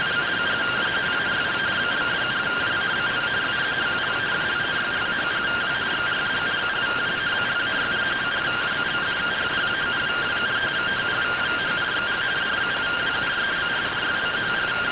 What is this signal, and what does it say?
Two-tone teleprinter signal, about 170 Hz shift.